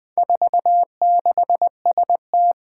Morse code, heard as a keyed tone, about 20 wpm, 700 Hz. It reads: 46ST
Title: Four Challenging Letter-Number Combinations 20wpm